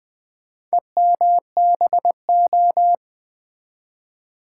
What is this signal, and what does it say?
Morse code, heard as a keyed tone, about 20 wpm, 700 Hz. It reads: EMBO
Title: Build A Word - 2 Letters At A Time - 6-Letter Words - 20wpm